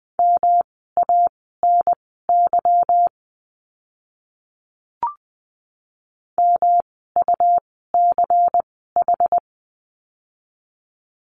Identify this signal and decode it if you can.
Morse code, heard as a keyed tone, about 20 wpm, spaced as for 14 wpm, 700 Hz. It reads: MANY MUCH
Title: 230 Lesson 37f - Letter x - Words 20wpm